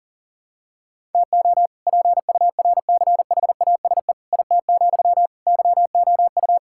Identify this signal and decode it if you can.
Morse code, heard as a keyed tone, about 37 wpm, 700 Hz. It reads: TO PURCHASE IT, YOU
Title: Sentences from Top 1000 Words - Review Part 5 - 40wpm